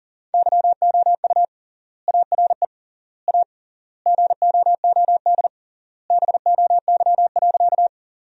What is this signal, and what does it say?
Morse code, heard as a keyed tone, about 40 wpm, 700 Hz. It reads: YOU ARE A GOOD BOY.
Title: 3x Word-Spacing - Sentences from Top 200 Words - 40wpm